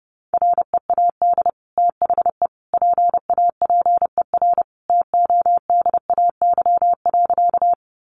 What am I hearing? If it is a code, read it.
Morse code, heard as a keyed tone, about 30 wpm, 700 Hz. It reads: READ THE PAPER TODAY.
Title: Sentences from Top 600 Words 30wpm